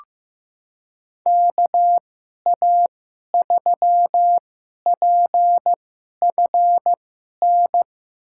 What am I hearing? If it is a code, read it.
Morse code, heard as a keyed tone, about 15 wpm, spaced as for 11 wpm, 700 Hz. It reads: KA3PFN
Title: USA Call Signs - 6 Characters - 2x Character Spacing - 15wpm